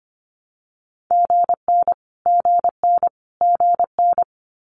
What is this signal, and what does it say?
Morse code, heard as a keyed tone, about 25 wpm, 700 Hz. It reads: GN GN GN
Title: Multi-letter Phonemes Repeated 3x 25wpm